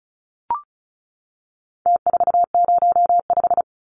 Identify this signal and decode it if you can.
Morse code, heard as a keyed tone, about 34 wpm, 700 Hz. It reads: T405
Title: Numbers - Contest Style - 4-Digit 35wpm